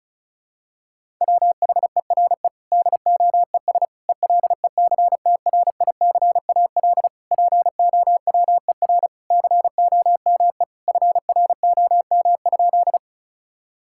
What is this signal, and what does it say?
Morse code, heard as a keyed tone, about 36 wpm, 700 Hz. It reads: WHERE DOES ELECTRICAL POWER COME FROM?
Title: Sentences from Top 1000 Words - Review Part 6 - 35wpm